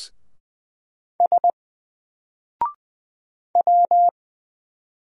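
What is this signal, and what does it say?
Morse code, heard as a keyed tone, about 20 wpm, 700 Hz. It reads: S W
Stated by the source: Single letter and number Morse drill at 20 wpm, from N4LCD.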